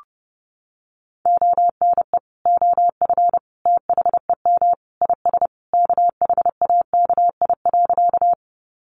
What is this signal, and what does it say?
Morse code, heard as a keyed tone, about 30 wpm, 700 Hz. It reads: ONE OF THEM IS KHAKI.
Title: Additional Easy Phrases 30wpm